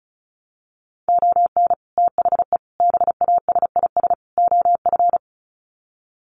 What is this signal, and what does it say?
Morse code, heard as a keyed tone, about 34 wpm, 700 Hz. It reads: ON THE BASIS OF
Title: Sets of 4 Words 35wpm